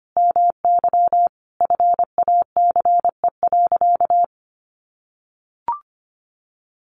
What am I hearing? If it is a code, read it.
Morse code, heard as a keyed tone, about 25 wpm, 700 Hz. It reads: MY FACE.
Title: Sentences from Top 1100 Words 25wpm